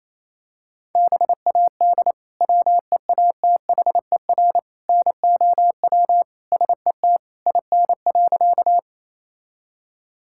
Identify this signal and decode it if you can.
Morse code, heard as a keyed tone, about 28 wpm, 700 Hz. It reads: BAD WEATHER NOW SET IN.
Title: Sentences from Top 1000 Words - Review Part 5 - 28wpm